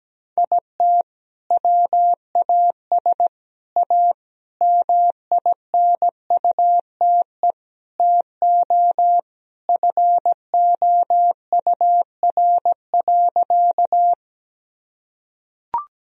Morse code, heard as a keyed tone, about 17 wpm, 700 Hz. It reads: IT WAS A MINUTE TO FOUR.
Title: Sentences from Top 1000 Words - Review Part 1 - 17wpm